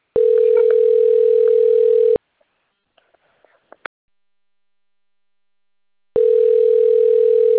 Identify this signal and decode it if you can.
Ringback tone (North America), 440 + 480 Hz.